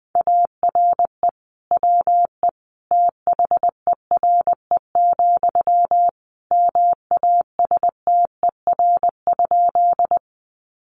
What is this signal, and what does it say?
Morse code, heard as a keyed tone, about 20 wpm, 700 Hz. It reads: ARE WE THERE, MASTER?
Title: Sentences from Top 1300 Words 20wpm